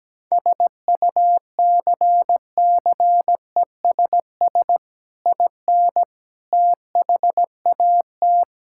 Morse code, heard as a keyed tone, about 17 wpm, 700 Hz. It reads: SUCCESS IN THAT
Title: Sentences from Top 1000 Words - Review Part 5 - 17wpm